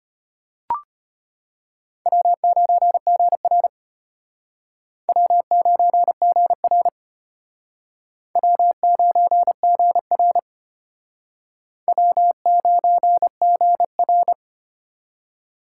Morse code, heard as a keyed tone, about 32 wpm, 700 Hz. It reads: W9GR W9GR W9GR W9GR